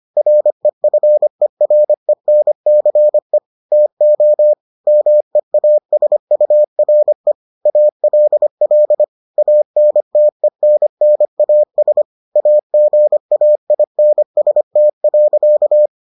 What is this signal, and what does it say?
Morse code, heard as a keyed tone, about 25 wpm, 600 Hz. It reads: REFERENCE TO MEASURE ALL ANTENNAS AGAINST.